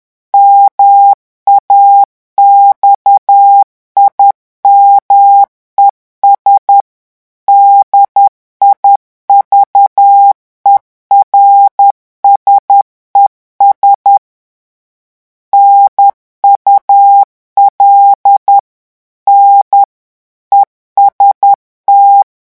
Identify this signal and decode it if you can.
Morse code, heard as a keyed tone, about 10 wpm, 800 Hz. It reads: MAXIMES DIVERSES NUL N EST
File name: morse mp3 / revision francais 10 mots minute